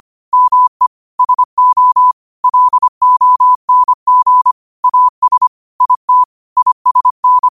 Morse code, heard as a keyed tone, about 25 wpm, 1000 Hz. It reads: ME SO LONG AS IT ISN